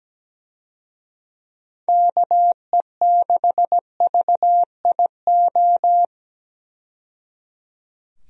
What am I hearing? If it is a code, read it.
Morse code, heard as a keyed tone, about 17 wpm, 700 Hz. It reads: KE6VIO
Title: USA Call Signs - 6 Characters - 17wpm